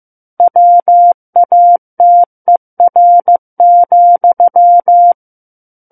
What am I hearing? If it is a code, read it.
Morse code, heard as a keyed tone, about 15 wpm, 700 Hz. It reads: WATER,
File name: Chapter 6 - 15 WPM